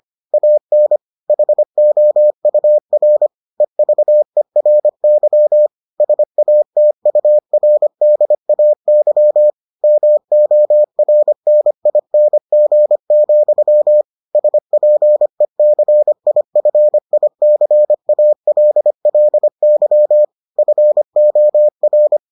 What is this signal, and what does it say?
Morse code, heard as a keyed tone, about 25 wpm, 600 Hz. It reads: AN HOUR EVERY SATURDAY MORNING, SPECIFICALLY FOR